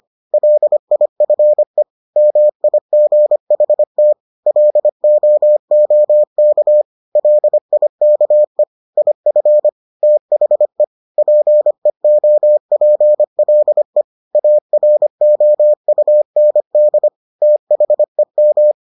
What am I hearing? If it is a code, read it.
Morse code, heard as a keyed tone, about 25 wpm, 600 Hz. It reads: LIFE MIGHT LOOK LIKE IF THE PEOPLE AROUND THEM